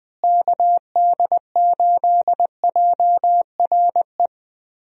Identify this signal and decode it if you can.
Morse code, heard as a keyed tone, about 20 wpm, 700 Hz. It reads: KD8JRE
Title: USA Call Signs - 6 Characters - Repeated 3x - 20wpm